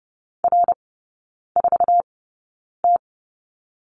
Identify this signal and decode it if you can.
Morse code, heard as a keyed tone, about 30 wpm, spaced as for 9 wpm, 700 Hz. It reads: R4T
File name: Lesson 16e - Letter c - Three Characters 30wpm